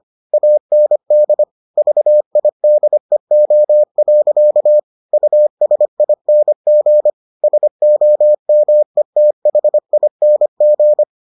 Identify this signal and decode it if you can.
Morse code, heard as a keyed tone, about 25 wpm, 600 Hz. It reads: AND VIDEO. USING SOMETHING